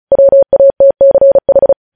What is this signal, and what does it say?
Morse code, heard as a keyed tone, about 36 wpm, 550 Hz. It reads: WATCH
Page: News headlines in Morse code at 35 wpm